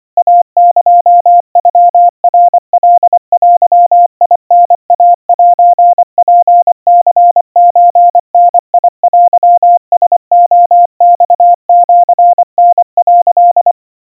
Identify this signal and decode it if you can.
Morse code (Japanese Wabun code), heard as a keyed tone, about 24 wpm, 700 Hz. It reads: イエノナカデタイセツニソダテラレマシタ」
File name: kawa / Morse sample / kamyu 90 cpm 700 Hz Crispy